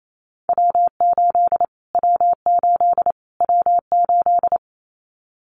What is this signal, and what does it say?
Morse code, heard as a keyed tone, about 28 wpm, 700 Hz. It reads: W8 W8 W8
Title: Callsign Prefixes for Top 25 DXCC - Repeated 3x - 28wpm